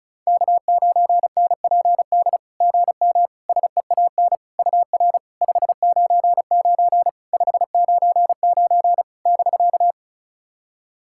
Morse code, heard as a keyed tone, about 36 wpm, 700 Hz. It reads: K9NPD GM SEAN UR 599 599 <BK>
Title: SOTA QSO Practice 35wpm